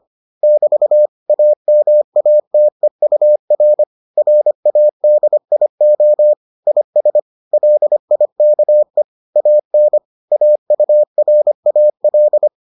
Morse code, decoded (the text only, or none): <BT> AMATEUR RADIO IS LIKE AN AURAL